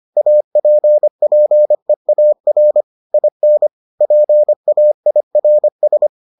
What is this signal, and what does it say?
Morse code, heard as a keyed tone, about 25 wpm, 600 Hz. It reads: APPEAR IN PAIRS